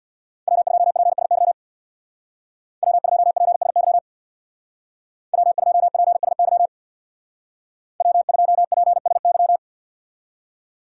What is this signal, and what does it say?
Morse code, heard as a keyed tone, about 65 wpm, 700 Hz. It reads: W2PSY W2PSY W2PSY W2PSY